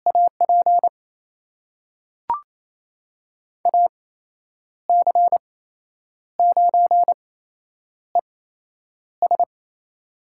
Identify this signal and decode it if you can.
Morse code, heard as a keyed tone, about 28 wpm, spaced as for 9 wpm, 700 Hz. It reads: AP AC9ES
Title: USA Call Signs - 5 Characters - 8x Character Spacing - 28wpm